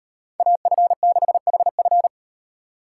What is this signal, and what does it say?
Morse code, heard as a keyed tone, about 38 wpm, 700 Hz. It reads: AF6HF